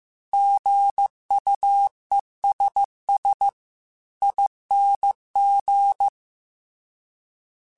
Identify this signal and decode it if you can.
Morse code, heard as a keyed tone, about 15 wpm, 800 Hz. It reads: GUESSING
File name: Alphabetical / Alphabetical mp3s / Gs @13